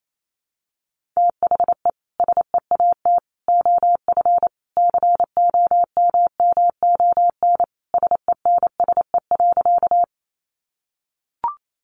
Morse code, heard as a keyed tone, about 28 wpm, 700 Hz. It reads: THE SEAT OF COMMON SENSE.
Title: Sentences from Top 1500 Words 28wpm